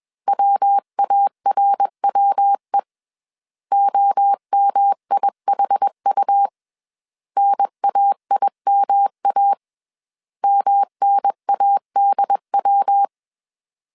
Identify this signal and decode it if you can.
Morse code, heard as a keyed tone, about 21 wpm, 800 Hz. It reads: WARWE OMIHU NAIMA MNADW